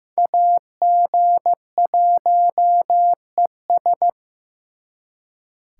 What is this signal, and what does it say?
Morse code, heard as a keyed tone, about 18 wpm, 700 Hz. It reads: AG1ES
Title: Speed Racing - USA 2x2 Call Signs 15wpm